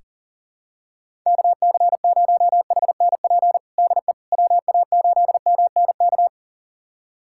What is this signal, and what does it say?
Morse code, heard as a keyed tone, about 40 wpm, 700 Hz. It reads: KC0HNP DE WA8MNK
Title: USA Callsign DE Callsign 40wpm